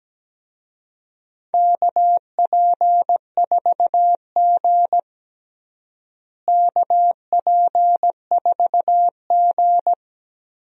Morse code, heard as a keyed tone, about 17 wpm, 700 Hz. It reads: KP4G KP4G
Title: DXCC-100 Call Signs - 4 Characters - Repeated 2x - 17wpm